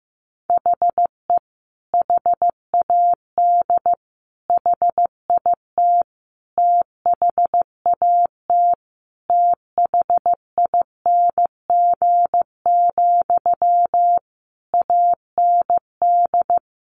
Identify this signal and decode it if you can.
Morse code, heard as a keyed tone, about 15 wpm, 700 Hz. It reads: HE HAD HIT THAT THING, AND
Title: Sentences from Top 1200 Words 15wpm